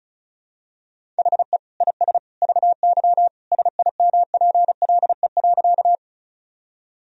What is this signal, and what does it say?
Morse code, heard as a keyed tone, about 36 wpm, 700 Hz. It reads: HE IS VY SIMPLE.
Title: Sentences from Top 400 Words and Abbreviations 35wpm